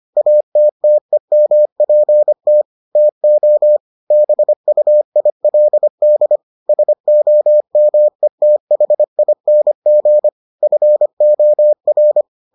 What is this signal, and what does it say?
Morse code, heard as a keyed tone, about 25 wpm, 600 Hz. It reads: ATTEMPT TO BUILD SOMETHING FOR